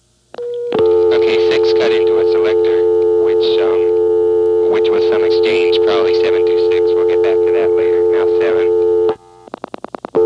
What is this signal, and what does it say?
Dial tone (North America), 350 + 440 Hz, with speech.
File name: MoreheadCityNCNoTest